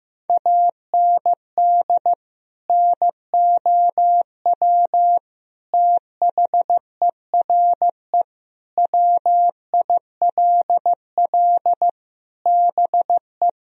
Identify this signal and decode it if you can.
Morse code, heard as a keyed tone, about 15 wpm, 700 Hz. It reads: AND NOW THERE WILL BE
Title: Sentences from Top 1000 Words - Review Part 2 - 15wpm